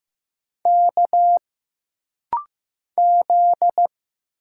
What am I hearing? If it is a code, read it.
Morse code, heard as a keyed tone, about 15 wpm, 700 Hz. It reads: K Z
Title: Single Letter-Number Alt - Rapid-Fire 15wpm